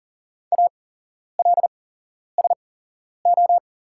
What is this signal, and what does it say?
Morse code, heard as a keyed tone, about 40 wpm, spaced as for 11 wpm, 700 Hz. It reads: ALSO
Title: 8x Character Spacing - Top 200 Words - 40wpm